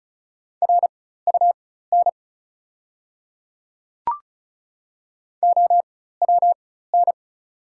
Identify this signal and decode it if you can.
Morse code, heard as a keyed tone, about 36 wpm, spaced as for 16 wpm, 690 Hz. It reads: RUN OWN
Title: 4x Character Spacing - Top 300 Words - 35wpm